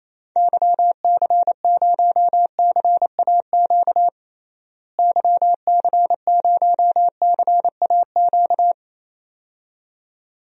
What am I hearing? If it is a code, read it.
Morse code, heard as a keyed tone, about 28 wpm, 700 Hz. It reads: YC0CAQ YC0CAQ
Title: DXCC-100 Call Signs - 6 Characters - Repeated 2x - 28wpm